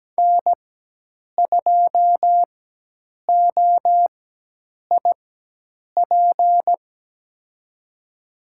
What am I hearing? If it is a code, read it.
Morse code, heard as a keyed tone, about 17 wpm, spaced as for 8 wpm, 700 Hz. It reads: N2OIP
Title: USA Call Signs - 5 Characters - 4x Character Spacing - 17wpm